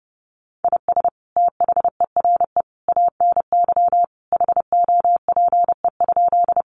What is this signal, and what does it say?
Morse code, heard as a keyed tone, about 28 wpm, 700 Hz. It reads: IS THERE ANY HOPE?
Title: Sentences from Top 1000 Words 30wpm